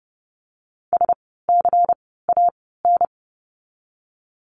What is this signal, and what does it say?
Morse code, heard as a keyed tone, about 32 wpm, 700 Hz. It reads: S C A N
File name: Lesson 16f - Letter c - Words 30wpm